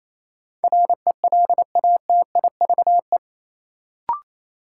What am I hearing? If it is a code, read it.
Morse code, heard as a keyed tone, about 28 wpm, 700 Hz. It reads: RELATIVE
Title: Sentences from Top 1400 Words - New Word Drill - 28wpm